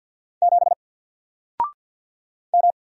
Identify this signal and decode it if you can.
Morse code, heard as a keyed tone, about 50 wpm, 700 Hz. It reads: 7 M
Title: Single Letter-Number Alt 50wpm